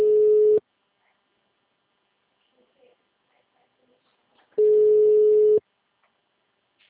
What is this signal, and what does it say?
Ringback tone (Europe, 425 Hz).
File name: out- -2003- - - .32790